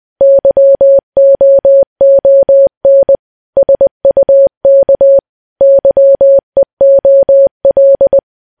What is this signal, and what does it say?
Morse code, heard as a keyed tone, about 20 wpm, 550 Hz. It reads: YOON SUK YEOL
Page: news-headlines-in-morse-code-at-20-wpm